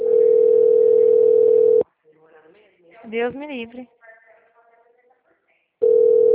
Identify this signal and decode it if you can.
Ringback tone (North America), 440 + 480 Hz, with speech.
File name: out- -2045- - - .37370